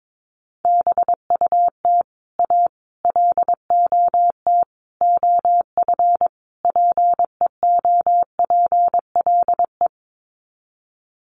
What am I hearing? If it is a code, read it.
Morse code, heard as a keyed tone, about 22 wpm, 700 Hz. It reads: BUT A LOT OF PEOPLE
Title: Sets of 5 Words Encore2 22wpm